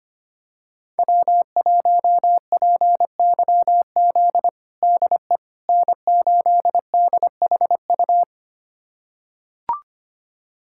Morse code, heard as a keyed tone, about 25 wpm, 700 Hz. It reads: W1PYZ DE N8DHU
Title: USA Callsign DE Callsign - 5 Characters - 25wpm